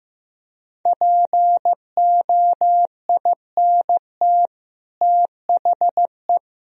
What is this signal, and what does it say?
Morse code, heard as a keyed tone, about 15 wpm, 700 Hz. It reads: POINT THE
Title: Sentences from Top 1000 Words - Review Part 2 - 15wpm